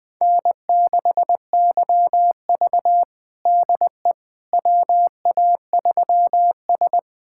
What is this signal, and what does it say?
Morse code, heard as a keyed tone, about 21 wpm, 700 Hz. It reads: N6YV DE WA3S
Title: USA Callsign DE Callsign - 4 Characters - 20wpm